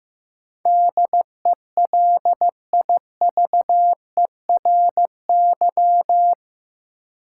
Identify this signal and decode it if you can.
Morse code, heard as a keyed tone, about 15 wpm, 700 Hz. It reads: DELIVERY
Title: Sentences from Top 1500 Words - New Word Drill Alt - 15wpm